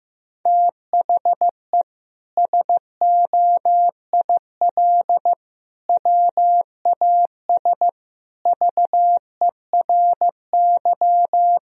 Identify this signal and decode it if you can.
Morse code, heard as a keyed tone, about 15 wpm, 700 Hz. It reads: THE SOIL WAS VERY
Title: Sentences from Top 1000 Words - Review Part 4 - 15wpm